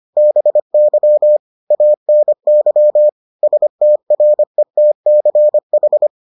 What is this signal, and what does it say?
Morse code, heard as a keyed tone, about 25 wpm, 600 Hz. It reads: BY ANY STRETCH